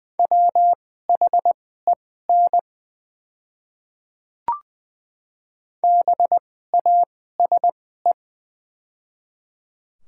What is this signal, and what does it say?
Morse code, heard as a keyed tone, about 20 wpm, spaced as for 15 wpm, 700 Hz. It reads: WHEN BASE
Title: 2x Character Spacing - Top 500 Words - 20wpm